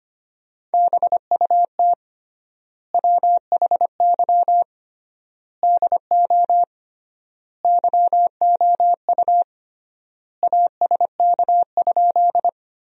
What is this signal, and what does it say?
Morse code, heard as a keyed tone, about 25 wpm, 700 Hz. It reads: BUT WHY DO YOU ASK?
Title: 3x Word-Spacing - Sentences from Top 200 Words - 25wpm